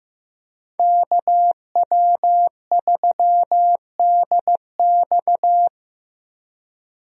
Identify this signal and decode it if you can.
Morse code, heard as a keyed tone, about 15 wpm, 700 Hz. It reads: KW3DX
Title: USA Call Signs - 5 Characters - Repeated 3x - 15wpm